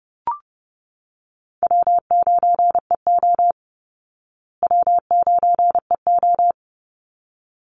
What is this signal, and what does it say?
Morse code, heard as a keyed tone, about 30 wpm, 700 Hz. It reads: W9EO W9EO
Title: Easy Call Signs Repeated 2x - 30wpm